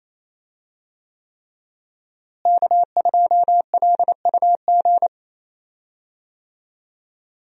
Morse code, heard as a keyed tone, about 28 wpm, 700 Hz. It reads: K2LUG
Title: USA Call Signs - 5 Characters - Repeated 3x - 28wpm